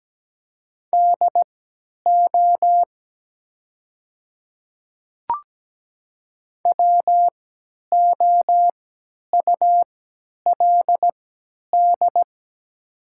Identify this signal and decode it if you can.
Morse code, heard as a keyed tone, about 17 wpm, spaced as for 10 wpm, 700 Hz. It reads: DO WOULD
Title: 3x Character Spacing - Top 100 Words - 17wpm